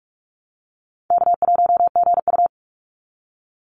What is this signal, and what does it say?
Morse code, heard as a keyed tone, about 44 wpm, 700 Hz. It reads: K1GU
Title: Easy Call Signs Repeated 2x - 45wpm